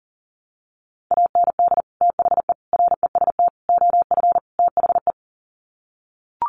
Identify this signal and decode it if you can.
Morse code, heard as a keyed tone, about 40 wpm, 700 Hz. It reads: AND THE REST OF THE
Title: Sets of 5 Words 40wpm